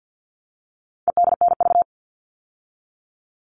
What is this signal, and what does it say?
Morse code, heard as a keyed tone, about 50 wpm, 700 Hz. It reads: EDNV
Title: All Cut Numbers - 4-Digit 50wpm